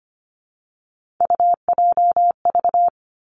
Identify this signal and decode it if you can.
Morse code, heard as a keyed tone, about 25 wpm, 700 Hz. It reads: UJV
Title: Three Challenging Letter-Number Combinations 25wpm